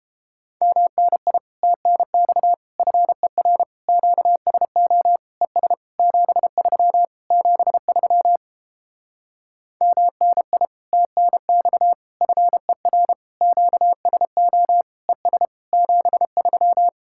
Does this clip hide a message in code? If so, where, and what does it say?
Yes: Morse code from 0.6 to 16.9 s, heard as a keyed tone, about 29 wpm, 700 Hz. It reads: MNI TNX FER QSO ES 73 73 MNI TNX FER QSO ES 73